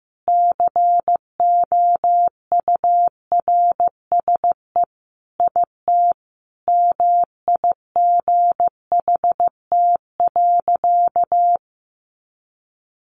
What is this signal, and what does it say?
Morse code, heard as a keyed tone, about 15 wpm, 700 Hz. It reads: COURSE IT MIGHT.
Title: Sentences from Top 500 Words 15wpm